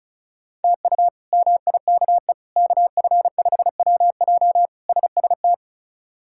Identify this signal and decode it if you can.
Morse code, heard as a keyed tone, about 36 wpm, 700 Hz. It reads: TU MIKE KF5WJ SST
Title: K1USN SST QSO Practice - 35wpm